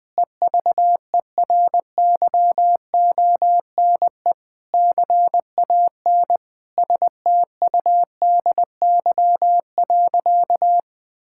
Morse code, heard as a keyed tone, about 21 wpm, 700 Hz. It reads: EVERYONE CAN STUDY.
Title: Sentences from Top 1000 Words - Review Part 5 - 20wpm